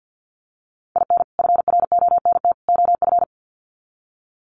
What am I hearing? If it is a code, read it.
Morse code, heard as a keyed tone, about 48 wpm, 700 Hz. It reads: IN FRONT OF
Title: Sets of 3 Words 50wpm